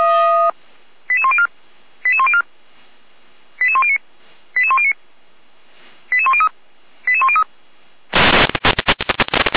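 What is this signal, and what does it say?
Five-tone selective call (ZVEI), heard as quick runs of five tones, tones of about 70 ms. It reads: 88104 88104 88109 88109 88103 88103
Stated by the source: ZVEI five-tone selective call received on an SDR.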